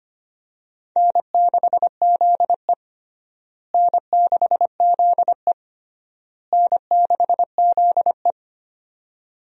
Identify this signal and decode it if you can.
Morse code, heard as a keyed tone, about 25 wpm, 700 Hz. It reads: N6ZE N6ZE N6ZE
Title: USA Call Signs - 4 Characters - Repeated 3x - 25wpm